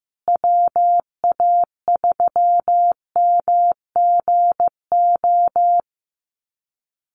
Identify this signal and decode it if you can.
Morse code, heard as a keyed tone, about 14 wpm, 700 Hz. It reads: WA3MGO
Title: USA Call Signs Repeated 3x 15wpm